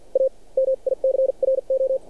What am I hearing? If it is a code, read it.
Morse code, heard as a keyed tone, about 53 wpm, spaced as for 42 wpm, 530 Hz. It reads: A MICRO
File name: archived / CRSnet